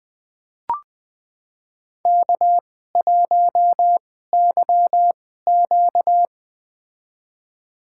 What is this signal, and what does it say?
Morse code, heard as a keyed tone, about 20 wpm, spaced as for 15 wpm, 700 Hz. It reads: K1YQ
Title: USA Call Signs - 4 Characters - 2x Character Spacing - 20wpm